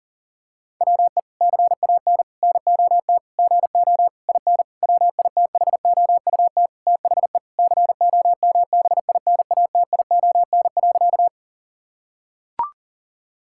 Morse code, heard as a keyed tone, about 40 wpm, 700 Hz. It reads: WE CAN NOT GO IN WITHOUT THE COMBINATION.
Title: Sentences from Top 1000 Words - Review Part 6 - 40wpm